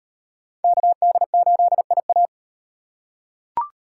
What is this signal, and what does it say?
Morse code, heard as a keyed tone, about 37 wpm, 700 Hz. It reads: KD8IA